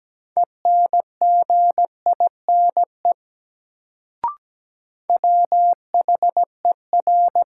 Morse code, heard as a keyed tone, about 17 wpm, 700 Hz. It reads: ENGINE WHER
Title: Sentences from Top 1500 Words - New Word Drill Alt - 17wpm